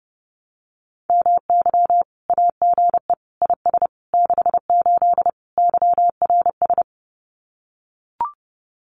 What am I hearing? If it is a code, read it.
Morse code, heard as a keyed tone, about 30 wpm, 700 Hz. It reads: MY AGE IS 68 YRS
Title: Common QSO Phrases Encore 30wpm